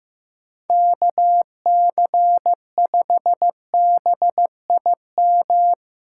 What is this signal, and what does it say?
Morse code, heard as a keyed tone, about 16 wpm, 700 Hz. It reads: KC5BIM
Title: USA Callsign DE Callsign 15wpm